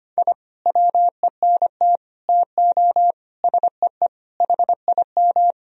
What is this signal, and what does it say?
Morse code, heard as a keyed tone, about 25 wpm, 700 Hz. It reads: I WENT TO SEE HIM
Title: Sentences from Top 1000 Words - Review Part 1 - 25wpm